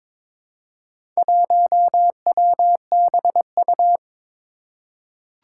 Morse code, heard as a keyed tone, about 22 wpm, 700 Hz. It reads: 1WBU
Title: Four Challenging Letter-Number Combinations 22wpm